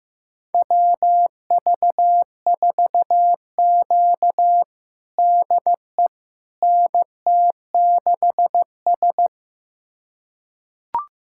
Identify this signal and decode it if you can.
Morse code, heard as a keyed tone, about 15 wpm, 700 Hz. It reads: WV4Q DE NT6S
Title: USA Callsign DE Callsign - 4 Characters - 15wpm